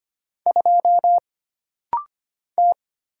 Morse code, heard as a keyed tone, about 25 wpm, 700 Hz. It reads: 2 T
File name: Single Character (Letter:Number:Punctuation:Prosign) - Rapid-Fire - 25wpm